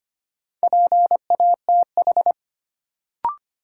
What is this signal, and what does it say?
Morse code, heard as a keyed tone, about 25 wpm, 700 Hz. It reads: PATH
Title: Sentences from Top 1400 Words - New Word Drill Alt - 25wpm